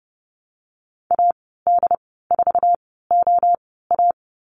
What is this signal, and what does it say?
Morse code, heard as a keyed tone, about 30 wpm, spaced as for 17 wpm, 700 Hz. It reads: AD4OA
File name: Lesson 15g - Number 5 - Call Signs 30wpm